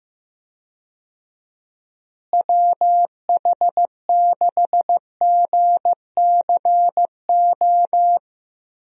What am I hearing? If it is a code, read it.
Morse code, heard as a keyed tone, about 15 wpm, 700 Hz. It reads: WH6GCO
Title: USA Call Signs - 6 Characters - Repeated 3x - 15wpm